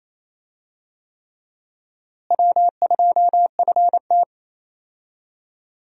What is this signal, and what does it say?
Morse code, heard as a keyed tone, about 28 wpm, 700 Hz. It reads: W2FT
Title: USA Call Signs - 4 Characters - Repeated 3x - 28wpm